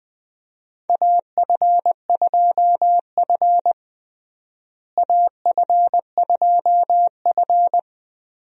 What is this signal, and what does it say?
Morse code, heard as a keyed tone, about 20 wpm, 700 Hz. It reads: AF2F AF2F
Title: USA Call Signs - 4 Characters - Repeated 3x - 20wpm